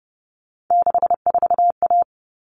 Morse code, heard as a keyed tone, about 29 wpm, 700 Hz. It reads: 64A
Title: All Cut Numbers - 3-Digit 30wpm